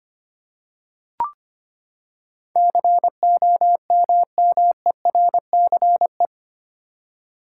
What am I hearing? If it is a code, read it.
Morse code, heard as a keyed tone, about 25 wpm, 700 Hz. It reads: COMMERCE
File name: Eight Letter Words - 25wpm